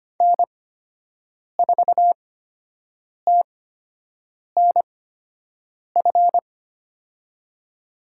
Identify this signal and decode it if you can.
Morse code, heard as a keyed tone, about 25 wpm, spaced as for 7 wpm, 700 Hz. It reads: N4TNF
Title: USA Call Signs - 8x Character Spacing - 25wpm